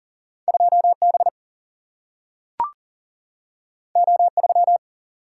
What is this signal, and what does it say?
Morse code, heard as a keyed tone, about 40 wpm, 700 Hz. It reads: 2 B O 3
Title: Single-Letter Single-Number 40wpm